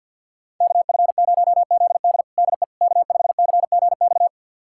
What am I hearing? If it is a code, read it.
Morse code, heard as a keyed tone, about 46 wpm, 700 Hz. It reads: KF0ZD DE K5CGX
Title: USA Callsign DE Callsign - 5 Characters - 50wpm